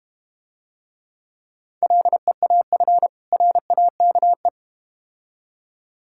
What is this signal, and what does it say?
Morse code, heard as a keyed tone, about 32 wpm, 700 Hz. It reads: LEAF RAKE